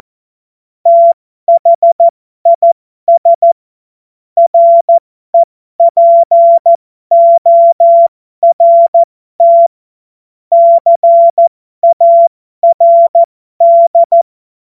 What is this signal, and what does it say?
Morse code, heard as a keyed tone, about 14 wpm, spaced as for 12 wpm, 680 Hz. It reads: THIS REPORT CARD
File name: Morsekurs / Witze